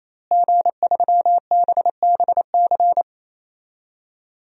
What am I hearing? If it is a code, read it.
Morse code, heard as a keyed tone, about 28 wpm, 700 Hz. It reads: G3BBC
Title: DXCC-100 Call Signs - 5 Characters - Repeated 3x - 28wpm